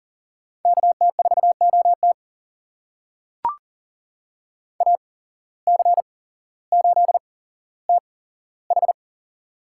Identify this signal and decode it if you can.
Morse code, heard as a keyed tone, about 39 wpm, spaced as for 17 wpm, 700 Hz. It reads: KT4OT AC8TH
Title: USA Call Signs - 5 Characters - 8x Character Spacing - 40wpm